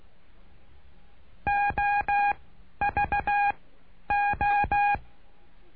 Morse code, heard as a keyed tone, about 16 wpm, spaced as for 10 wpm, 820 Hz. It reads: OVO
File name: cursos / CURSO DE TELEGRAFIA / Telegrafia - Trilha 09